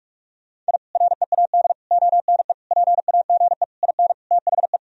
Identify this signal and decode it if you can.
Morse code, heard as a keyed tone, about 46 wpm, 700 Hz. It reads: I READ ONE PAGE IN THE
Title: Sentences from Top 1000 Words - Review Part 1 - 45wpm